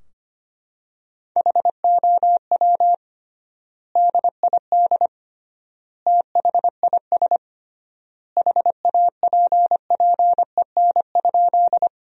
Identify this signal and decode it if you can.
Morse code, heard as a keyed tone, about 25 wpm, 700 Hz. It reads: HOW DID THIS HAPPEN?
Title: 3x Word-Spacing - Sentences from Top 400 Words - 25wpm